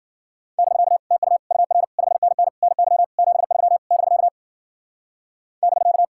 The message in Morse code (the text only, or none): <BK> TU UR 5NN NY 73 <BK> <BK>